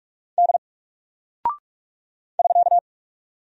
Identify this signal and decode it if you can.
Morse code, heard as a keyed tone, about 48 wpm, 700 Hz. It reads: D <SK>
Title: Single Character (Letter:Number:Punctuation:Prosign) Alt 45wpm